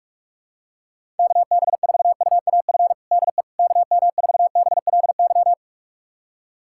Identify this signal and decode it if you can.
Morse code, heard as a keyed tone, about 46 wpm, 700 Hz. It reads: KB4UAF DE KM4BLY
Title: USA Callsign DE Callsign 45wpm